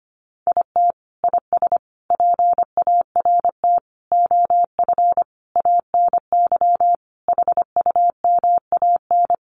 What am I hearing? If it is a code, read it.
Morse code, heard as a keyed tone, about 25 wpm, 700 Hz. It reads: IT IS PART OF ANY HUMAN
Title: Sentences from Top 1100 Words 25wpm